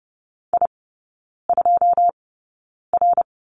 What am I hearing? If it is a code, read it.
Morse code, heard as a keyed tone, about 30 wpm, spaced as for 9 wpm, 700 Hz. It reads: I2R
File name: 081 Lesson 14e - Number 2 - Three Characters 30wpm